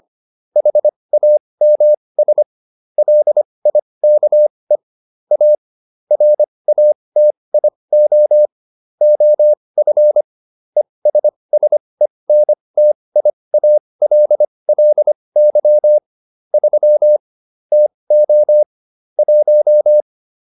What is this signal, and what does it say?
Morse code, heard as a keyed tone, about 25 wpm, spaced as for 20 wpm, 600 Hz. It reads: HAMS LIKE A RATIO OF ESSENTIALLY 3 TO 1
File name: SS316 20